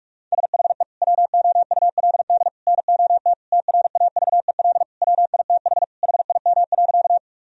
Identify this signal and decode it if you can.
Morse code, heard as a keyed tone, about 46 wpm, 700 Hz. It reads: SHE WOULD NOT TRAVEL WITH HIM.
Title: Sentences from Top 1000 Words - Review Part 2 - 45wpm